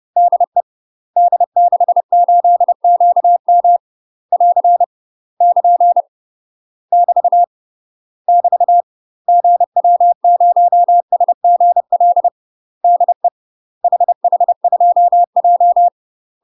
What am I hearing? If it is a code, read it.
Morse code, heard as a keyed tone, about 30 wpm, 700 Hz. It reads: DE D68QM <AR> <KN> <BT> <BT> GW0SGL DE HH2J
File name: UK MASTER QSOs TRAINING / 30WPM / UK MASTER QSOs 0024